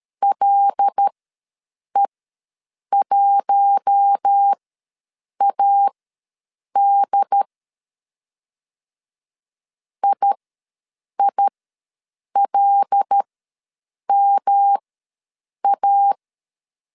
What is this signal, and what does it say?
Morse code, heard as a keyed tone, about 13 wpm, spaced as for 7 wpm, 800 Hz. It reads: LE1AD IILMA
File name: LEZIONI / 5 LBVC / 5 L ws 12wpm 2958